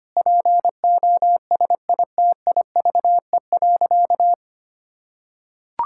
Morse code, heard as a keyed tone, about 25 wpm, 700 Hz. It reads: POSITIVE.
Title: Sentences from Top 1000 Words - Review Part 5 - 25wpm